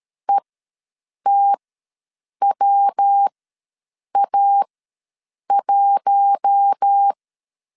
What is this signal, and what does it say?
Morse code, heard as a keyed tone, about 13 wpm, spaced as for 7 wpm, 800 Hz. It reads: ETWA1
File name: LEZIONI / 5 LBVC / 5 L ws 12wpm 2958